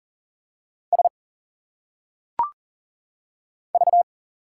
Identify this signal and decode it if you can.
Morse code, heard as a keyed tone, about 40 wpm, 700 Hz. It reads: S V
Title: Single Letters - Phonetic Pronunciation 40wpm